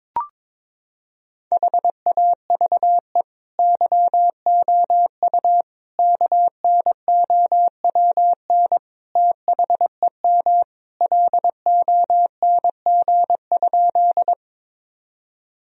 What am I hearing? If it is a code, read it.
Morse code, heard as a keyed tone, about 22 wpm, 700 Hz. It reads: HAVE YOU KNOWN THEM LONG?
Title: Sentences from Top 1000 Words - Review Part 3 - 22wpm